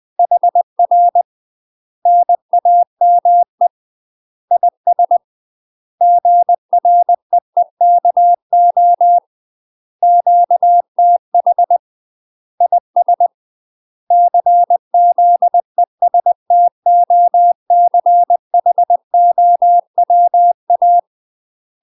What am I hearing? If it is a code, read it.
Morse code, heard as a keyed tone, about 20 wpm, 700 Hz. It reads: HR NAME IS GREEKO QTH IS CZESTOCHOWA
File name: UK MASTER QSOs TRAINING / 20WPM / UK MASTER QSOs 0046